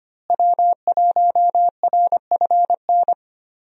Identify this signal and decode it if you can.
Morse code, heard as a keyed tone, about 25 wpm, 700 Hz. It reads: W1RFN
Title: USA Call Signs - 5 Characters - Repeated 3x - 25wpm